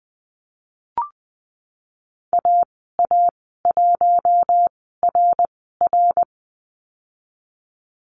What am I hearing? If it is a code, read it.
Morse code, heard as a keyed tone, about 19 wpm, spaced as for 14 wpm, 700 Hz. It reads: AA1RR
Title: 062 Lesson 11g - Letter h - Call Signs 20wpm